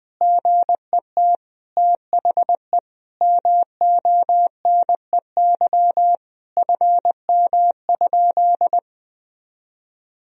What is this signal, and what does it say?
Morse code, heard as a keyed tone, about 20 wpm, 700 Hz. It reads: GET THE MONEY FM?
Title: Sentences from Top 400 Words and Abbreviations 20wpm